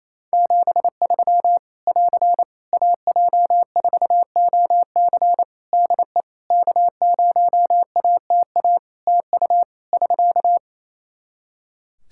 Morse code, heard as a keyed tone, about 28 wpm, 700 Hz. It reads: 73 <AR> AJ4OC DE K0ATA TU <SK>
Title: Practice QSO - Exchange 4 - Ending - 28wpm